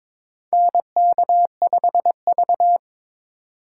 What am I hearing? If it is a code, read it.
Morse code, heard as a keyed tone, about 22 wpm, 700 Hz. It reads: NK5V
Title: USA Call Signs - 4 Characters - 4x Character Spacing - 22wpm